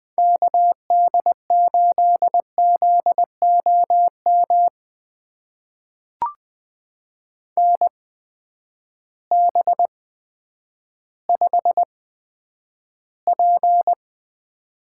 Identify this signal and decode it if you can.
Morse code, heard as a keyed tone, about 20 wpm, spaced as for 10 wpm, 700 Hz. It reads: KD8ZOM NB5P
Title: USA Call Signs - 8x Character Spacing - 20wpm